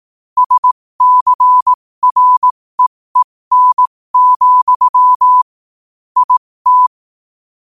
Morse code, heard as a keyed tone, about 18 wpm, spaced as for 15 wpm, 1000 Hz. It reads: SCREEN, IT
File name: qotdfiles / 15wpm